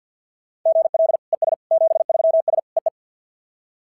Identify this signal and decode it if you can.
Morse code, heard as a keyed tone, about 52 wpm, 690 Hz. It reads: GL ES 73S EE
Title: SOTA QSO Practice 50wpm